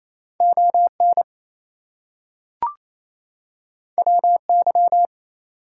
Morse code, heard as a keyed tone, about 28 wpm, 700 Hz. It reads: ON WY
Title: US States and Canadian Provinces 28wpm